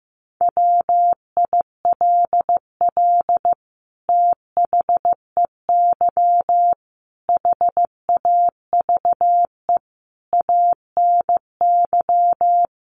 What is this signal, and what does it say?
Morse code, heard as a keyed tone, about 15 wpm, 700 Hz. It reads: WILL THEY HAVE ANY
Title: Sentences from Top 1500 Words 15wpm